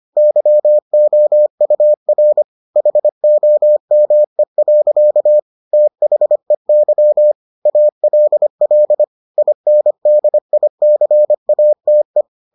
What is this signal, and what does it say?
Morse code, heard as a keyed tone, about 25 wpm, 600 Hz. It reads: YOUR HOME. THEY ALL INDICATE